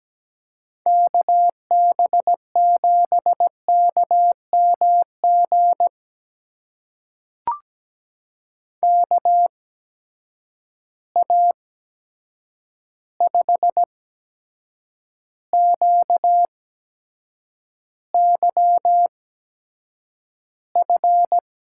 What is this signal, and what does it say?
Morse code, heard as a keyed tone, about 17 wpm, spaced as for 7 wpm, 700 Hz. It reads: KB7KMG KA5QYF
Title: USA Call Signs - 6 Characters - 8x Character Spacing - 17wpm